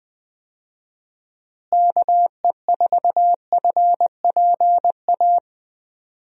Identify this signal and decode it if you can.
Morse code, heard as a keyed tone, about 20 wpm, 700 Hz. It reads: KE4FPA
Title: USA Call Signs - 6 Characters - Repeated 3x - 20wpm